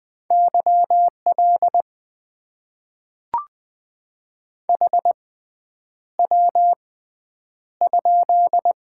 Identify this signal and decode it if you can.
Morse code, heard as a keyed tone, about 20 wpm, spaced as for 9 wpm, 700 Hz. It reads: YL HW?
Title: Top 90 QSO Elements - 6x Character Spacing - 20wpm